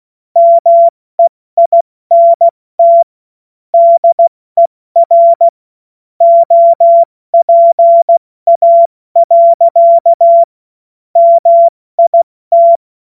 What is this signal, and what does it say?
Morse code, heard as a keyed tone, about 16 wpm, 680 Hz. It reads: MEINT DER OPA. MIT
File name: Morsekurs / Witze / Witz 19. Oktober 2021, 70 F80